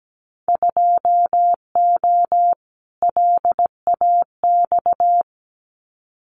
Morse code, heard as a keyed tone, about 17 wpm, 700 Hz. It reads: 2O LAX
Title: Winter Field Day Exchanges 17wpm